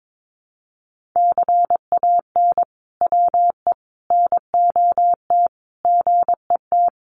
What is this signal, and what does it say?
Morse code, heard as a keyed tone, about 22 wpm, 700 Hz. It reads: CAN WE NOT GET
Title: Sentences from Top 1500 Words 22wpm